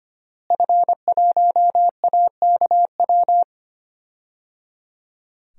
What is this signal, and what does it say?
Morse code, heard as a keyed tone, about 25 wpm, 700 Hz. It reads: F1AKW
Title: DXCC-100 Call Signs - 5 Characters - Repeated 3x - 25wpm